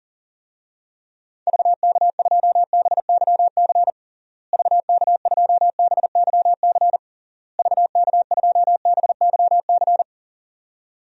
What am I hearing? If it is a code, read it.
Morse code, heard as a keyed tone, about 40 wpm, 700 Hz. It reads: VK2BYC VK2BYC VK2BYC
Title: DXCC-100 Call Signs - 6 Characters - Repeated 3x - 40wpm